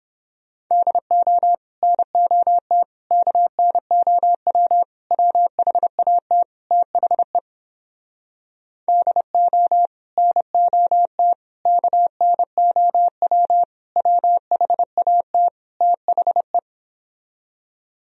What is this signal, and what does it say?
Morse code, heard as a keyed tone, about 31 wpm, 700 Hz. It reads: DO NOT KNOW WHAT THE DO NOT KNOW WHAT THE